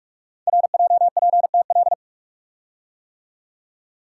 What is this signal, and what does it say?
Morse code, heard as a keyed tone, about 44 wpm, 700 Hz. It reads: RJPTL
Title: Code Groups - 5 Letters - 45wpm